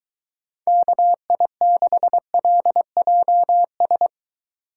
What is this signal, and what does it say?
Morse code, heard as a keyed tone, about 23 wpm, 700 Hz. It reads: KI6LJS